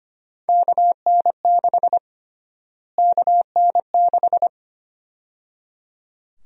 Morse code, heard as a keyed tone, about 25 wpm, 700 Hz. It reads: KN6 KN6
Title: USA Call Sign Prefixes - Repeated 2x - 25wpm